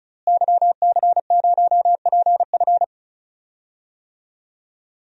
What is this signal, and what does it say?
Morse code, heard as a keyed tone, about 36 wpm, 700 Hz. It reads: YC0PF
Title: DXCC-100 Call Signs - 5 Characters - Repeated 2x - 35wpm